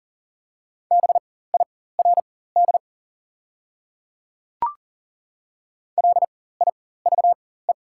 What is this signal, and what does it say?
Morse code, heard as a keyed tone, about 40 wpm, spaced as for 19 wpm, 700 Hz. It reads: BIRD LIVE
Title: 4x Character Spacing - Top 500 Words - 40wpm